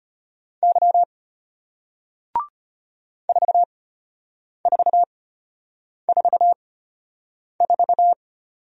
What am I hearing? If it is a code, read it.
Morse code, heard as a keyed tone, about 33 wpm, spaced as for 8 wpm, 700 Hz. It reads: Y 4444